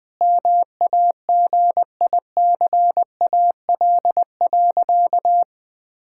Morse code, heard as a keyed tone, about 19 wpm, 700 Hz. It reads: MAGICAL.
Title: Sentences with 5-Word Phrases - 20wpm